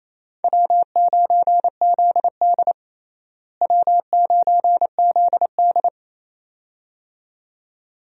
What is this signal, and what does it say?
Morse code, heard as a keyed tone, about 28 wpm, 700 Hz. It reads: W9ZD W9ZD
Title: USA Call Signs - 4 Characters - Repeated 2x - 28wpm